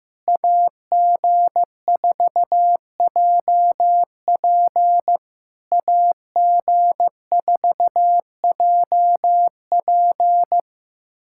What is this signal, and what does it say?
Morse code, heard as a keyed tone, about 15 wpm, 690 Hz. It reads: AG4JP AG4JP
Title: SOTA QSO Practice 15wpm